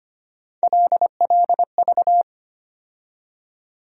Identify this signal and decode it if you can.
Morse code, heard as a keyed tone, about 25 wpm, 700 Hz. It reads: LLV
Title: USA Call Sign Postfixes - 3 Letters 25wpm